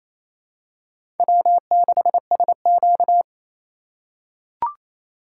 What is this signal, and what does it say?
Morse code, heard as a keyed tone, about 28 wpm, 700 Hz. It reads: W6SQ
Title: USA Call Signs - 4 Characters - 4x Character Spacing - 28wpm